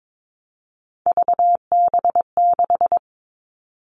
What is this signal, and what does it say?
Morse code, heard as a keyed tone, about 22 wpm, 700 Hz. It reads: VB6
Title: All Cut Numbers - 3-Digit 22wpm